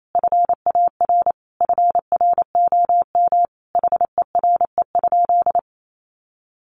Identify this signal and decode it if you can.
Morse code, heard as a keyed tone, about 28 wpm, 700 Hz. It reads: FAR FROM HERE?
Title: Sentences from Top 1100 Words 28wpm